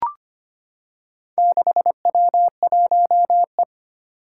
Morse code, heard as a keyed tone, about 25 wpm, 700 Hz. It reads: 6W1E
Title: Four Challenging Letter-Number Combinations 25wpm